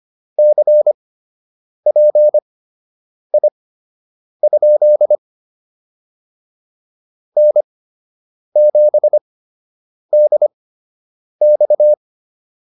Morse code, heard as a keyed tone, about 25 wpm, spaced as for 8 wpm, 600 Hz. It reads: CPI? N7DX